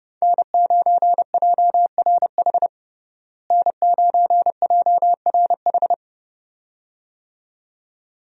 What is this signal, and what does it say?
Morse code, heard as a keyed tone, about 30 wpm, 700 Hz. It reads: N9JRH N9JRH
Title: USA Call Signs - 5 Characters - Repeated 2x - 30wpm